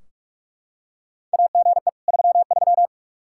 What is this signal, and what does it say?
Morse code, heard as a keyed tone, about 43 wpm, 700 Hz. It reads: AGE 33
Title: Practice QSO - Exchange 3 - More Info - 45wpm